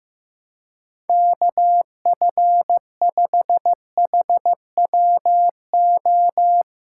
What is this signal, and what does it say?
Morse code, heard as a keyed tone, about 15 wpm, 700 Hz. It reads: KF5HWO
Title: USA Callsign DE Callsign - 6 Characters - 15wpm